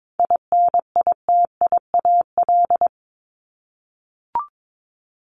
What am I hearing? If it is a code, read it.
Morse code, heard as a keyed tone, about 22 wpm, 700 Hz. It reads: INITIAL
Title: Top 1501-1750 Words 22wpm